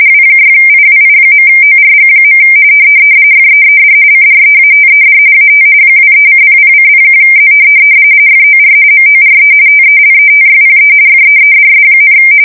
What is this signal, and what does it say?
RTTY, about 45.45 baud. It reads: W6IWI.ORG. FEED CREATED AT 23:00:01Z FOR MORE TELETYPE FEEDS,
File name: 60wpm / latest